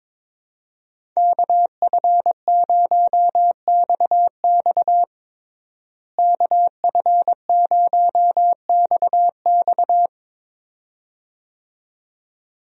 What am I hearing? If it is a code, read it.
Morse code, heard as a keyed tone, about 22 wpm, 700 Hz. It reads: KF0XX KF0XX
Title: USA Call Signs - 5 Characters - Repeated 2x - 22wpm